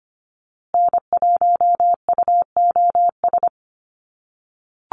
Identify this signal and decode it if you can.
Morse code, heard as a keyed tone, about 25 wpm, 700 Hz. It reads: N1UOS
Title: USA Call Signs Repeated 3x 25wpm